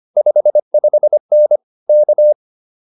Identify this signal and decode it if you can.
Morse code, heard as a keyed tone, about 25 wpm, 600 Hz. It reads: 55N K